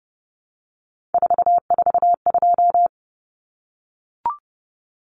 Morse code, heard as a keyed tone, about 30 wpm, 700 Hz. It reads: 442
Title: Numbers - 3-Digit 30wpm